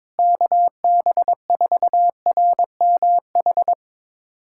Morse code, heard as a keyed tone, about 22 wpm, 700 Hz. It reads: KB4RMH
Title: USA Callsign DE Callsign 22wpm